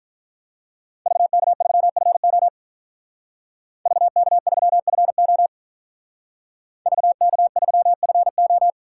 Morse code, heard as a keyed tone, about 43 wpm, 700 Hz. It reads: VK3FO VK3FO VK3FO
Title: Speed Racing - USA 2x2 Call Signs 35wpm